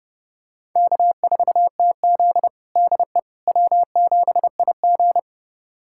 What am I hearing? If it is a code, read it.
Morse code, heard as a keyed tone, about 30 wpm, 700 Hz. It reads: K4TZ DE W7IG
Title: USA Callsign DE Callsign - 4 Characters - 30wpm